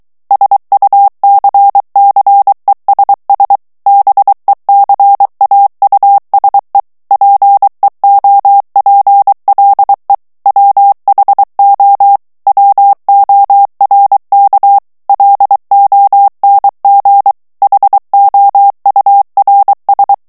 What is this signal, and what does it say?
Morse code, heard as a keyed tone, about 23 wpm, 800 Hz. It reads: SUCCESS BECAUSE PEOPLE WHO WORK LONG HOURS
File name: morse mp3 / ANGLAIS texte clair 1 vitesse 22 mots minute